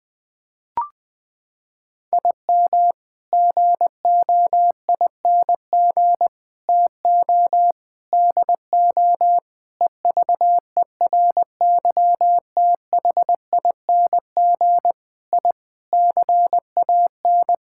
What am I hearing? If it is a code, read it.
Morse code, heard as a keyed tone, about 21 wpm, 700 Hz. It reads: IM GOING TO DO EVERYTHING I CAN
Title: Sets of 8 Words - 20wpm